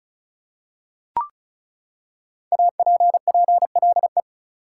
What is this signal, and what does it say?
Morse code, heard as a keyed tone, about 34 wpm, 700 Hz. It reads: APPLE
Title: Five Letter Words - 35wpm